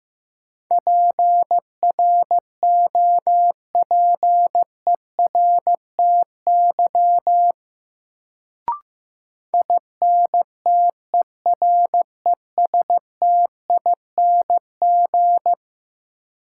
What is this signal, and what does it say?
Morse code, heard as a keyed tone, about 15 wpm, 700 Hz. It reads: PROPERTY INTERESTING
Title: Sentences from Top 1000 Words - New Word Drill - 15wpm